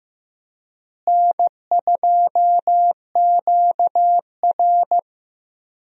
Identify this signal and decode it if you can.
Morse code, heard as a keyed tone, about 15 wpm, 700 Hz. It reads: N2QR
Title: USA Call Signs - 4 Characters - Repeated 3x - 15wpm